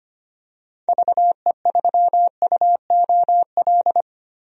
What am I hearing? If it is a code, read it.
Morse code, heard as a keyed tone, about 25 wpm, 700 Hz. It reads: VE3UOL
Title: DXCC-100 Call Signs - 6 Characters - 25wpm